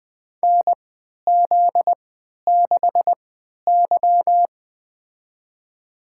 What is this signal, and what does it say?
Morse code, heard as a keyed tone, about 20 wpm, spaced as for 11 wpm, 700 Hz. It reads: NZ6Y
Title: USA Call Signs - 4 Characters - 3x Character Spacing - 20wpm